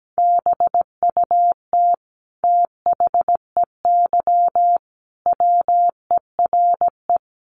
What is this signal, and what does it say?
Morse code, heard as a keyed tone, about 17 wpm, 700 Hz. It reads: BUT THEY WERE
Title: Sentences from Top 1400 Words 17wpm